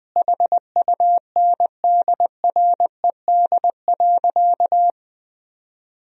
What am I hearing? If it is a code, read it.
Morse code, heard as a keyed tone, about 21 wpm, 700 Hz. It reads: HUNDRED.
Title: Sentences from Top 1000 Words - Review Part 2 - 20wpm